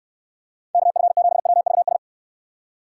Morse code, heard as a keyed tone, about 66 wpm, 700 Hz. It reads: KF7WVS